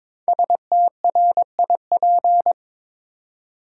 Morse code, heard as a keyed tone, about 23 wpm, 700 Hz. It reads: STRIP
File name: 5 Letter Words - First 3 Letters Given - 22wpm